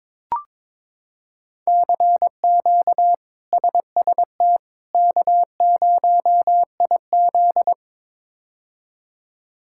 Morse code, heard as a keyed tone, about 22 wpm, 700 Hz. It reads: CQ SST K0IZ
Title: K1USN SST QSO Practice - 22wpm